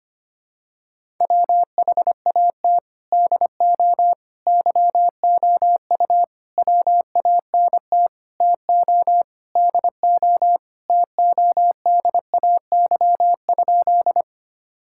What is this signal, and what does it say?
Morse code, heard as a keyed tone, about 25 wpm, 700 Hz. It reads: WHAT DO YOU WANT TO DO TODAY?
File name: practice / 25wpm / Sentences with 4-Word Phrases - 25wpm